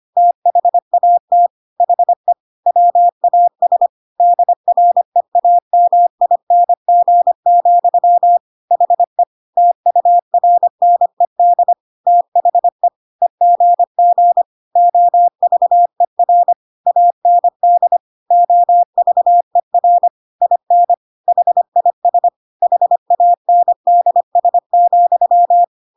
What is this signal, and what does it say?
Morse code, heard as a keyed tone, about 25 wpm, 700 Hz. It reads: THAT HE WAS DREAMING, HE TURNED THE EGG OVER AND OVER IN HIS HANDS,
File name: Pinocchio / Pinocchio ch5 25 25 ik1hge